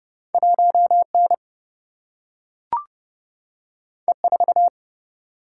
Morse code, heard as a keyed tone, about 30 wpm, 700 Hz. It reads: 1N E4
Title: Single-Letter Single-Number 30wpm